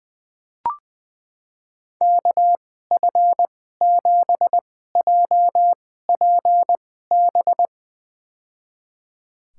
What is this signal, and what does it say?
Morse code, heard as a keyed tone, about 21 wpm, spaced as for 15 wpm, 700 Hz. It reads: KF7JPB
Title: USA Call Signs - 2x Character Spacing - 20wpm